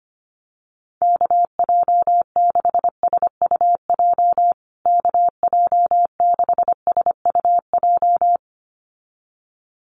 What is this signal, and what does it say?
Morse code, heard as a keyed tone, about 25 wpm, 700 Hz. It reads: KJ6SUJ KJ6SUJ
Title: USA Call Signs Repeated 2x 25wpm